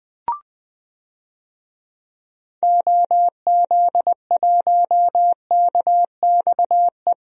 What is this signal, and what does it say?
Morse code, heard as a keyed tone, about 21 wpm, 700 Hz. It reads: OZ1KXE
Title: DXCC-100 Call Signs - 6 Characters - 20wpm